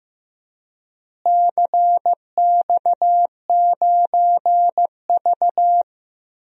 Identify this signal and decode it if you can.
Morse code, heard as a keyed tone, about 15 wpm, 700 Hz. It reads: CX9V
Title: DXCC-100 Call Signs - 4 Characters - Repeated 3x - 15wpm